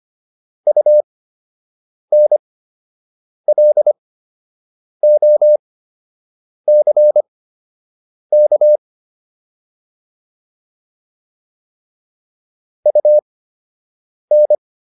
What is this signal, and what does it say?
Morse code, heard as a keyed tone, about 25 wpm, spaced as for 7 wpm, 600 Hz. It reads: UNLOCK UN